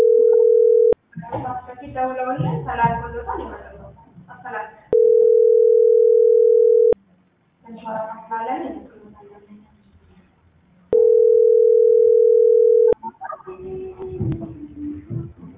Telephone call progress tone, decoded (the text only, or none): ringback tone